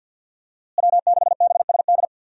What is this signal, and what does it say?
Morse code, heard as a keyed tone, about 53 wpm, 700 Hz. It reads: W6BSD
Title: USA Call Signs - 5 Characters - Repeated 3x - 50wpm